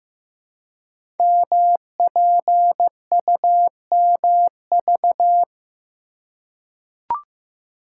Morse code, heard as a keyed tone, about 15 wpm, 700 Hz. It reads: MPUMV
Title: Code Groups - 5 Letters - 15wpm